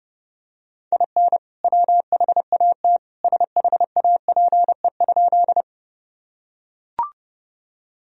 Morse code, heard as a keyed tone, about 32 wpm, 700 Hz. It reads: IN WHAT SHAPE?
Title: Sentences from Top 1000 Words - Review Part 3 - 30wpm